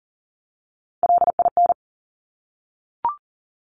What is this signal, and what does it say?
Morse code, heard as a keyed tone, about 42 wpm, 700 Hz. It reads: LIN
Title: Top Three Letter Combinations in Words 40wpm